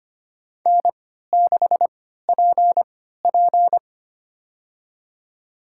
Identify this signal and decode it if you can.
Morse code, heard as a keyed tone, about 25 wpm, spaced as for 14 wpm, 700 Hz. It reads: N6PP
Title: USA Call Signs - 4 Characters - 3x Character Spacing - 25wpm